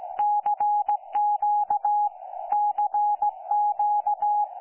Morse code, heard as a keyed tone, about 17 wpm, 830 Hz. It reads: CQ CQ
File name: 09 38Z 18088.0kHz k3fef darrel